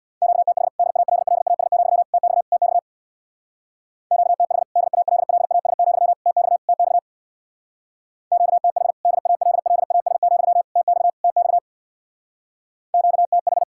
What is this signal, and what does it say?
Morse code, heard as a keyed tone, about 60 wpm, 700 Hz. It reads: QTH DALLAS, TX TX QTH DALLAS, TX TX QTH DALLAS, TX TX QTH
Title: Speed-Racing Common QSO Phrases - 50wpm